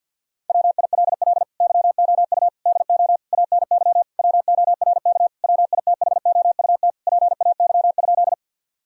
Morse code, heard as a keyed tone, about 48 wpm, 700 Hz. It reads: WILL YOU DO ANY WORK WITHOUT PAY?
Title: Sentences from Top 1000 Words - Review Part 3 - 50wpm